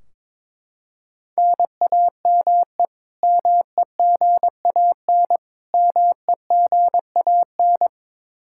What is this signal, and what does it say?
Morse code, heard as a keyed tone, about 21 wpm, 700 Hz. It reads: NAME MEGAN MEGAN
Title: Practice QSO - Exchange 1 - Beginning - 22wpm